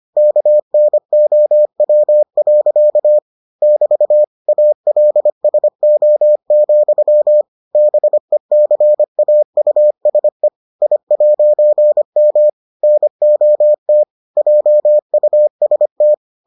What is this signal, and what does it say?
Morse code, heard as a keyed tone, about 25 wpm, 600 Hz. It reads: KNOW. <BT> ALSO, BECAUSE I'M NOT JUST